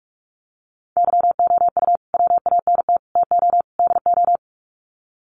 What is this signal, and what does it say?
Morse code, heard as a keyed tone, about 44 wpm, 700 Hz. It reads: YOU WANT TO DO
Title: Sets of 4 Words 45wpm